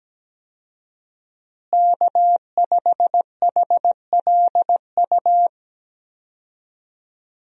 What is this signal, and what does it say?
Morse code, heard as a keyed tone, about 17 wpm, 700 Hz. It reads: K5HLU
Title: USA Call Signs - 5 Characters - Repeated 2x - 17wpm